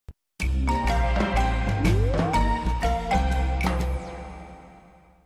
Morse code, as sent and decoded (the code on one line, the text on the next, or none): none